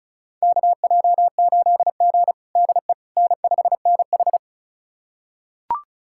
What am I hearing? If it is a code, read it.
Morse code, heard as a keyed tone, about 36 wpm, 700 Hz. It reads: KJ8G DE N5NH
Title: USA Callsign DE Callsign - 4 Characters - 35wpm